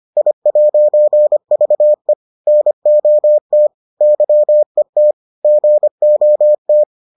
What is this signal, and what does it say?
Morse code, heard as a keyed tone, about 25 wpm, 600 Hz. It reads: I'VE NOT YET GOT